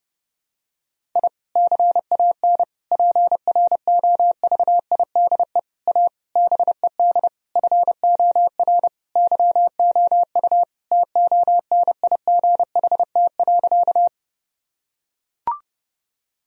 Morse code, heard as a keyed tone, about 30 wpm, 700 Hz. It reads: I CAN PROVIDE A BED FOR YOU TONIGHT.
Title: Sentences from Top 1000 Words - Review Part 3 - 30wpm